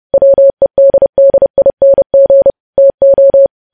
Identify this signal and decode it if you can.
Morse code, heard as a keyed tone, about 30 wpm, 550 Hz. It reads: WEDDING TO
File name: podcast / News headlines-1x-30x30